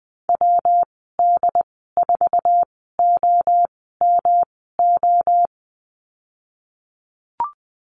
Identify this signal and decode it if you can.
Morse code, heard as a keyed tone, about 20 wpm, spaced as for 14 wpm, 700 Hz. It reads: WD4OMO
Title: 140 Lesson 23g - Letter f - Call Signs 20wpm